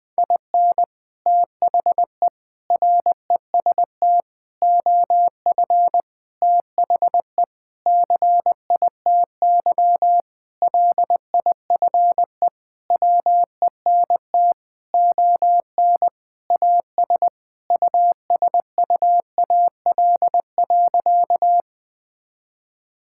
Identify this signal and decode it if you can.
Morse code, heard as a keyed tone, about 20 wpm, 700 Hz. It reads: IN THE REST OF THE CITY LIFE WENT ON AS USUAL.
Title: Sentences with 5-Word Phrases - 20wpm